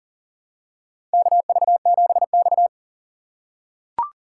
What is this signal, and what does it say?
Morse code, heard as a keyed tone, about 40 wpm, 700 Hz. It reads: KV7X
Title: USA Call Signs - 4 Characters - 4x Character Spacing - 40wpm